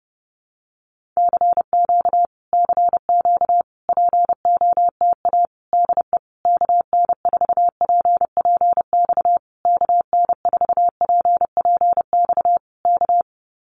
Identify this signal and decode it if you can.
Morse code, heard as a keyed tone, about 30 wpm, 700 Hz. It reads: CQ CQ POTA DE KN4PPX KN4PPX K
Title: Common QSO Phrases Encore 30wpm